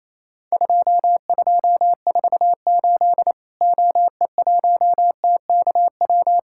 Morse code, heard as a keyed tone, about 28 wpm, 700 Hz. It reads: 2248 OE1TKW
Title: Worked All Europe DX Contest - QTC Exchange - 28wpm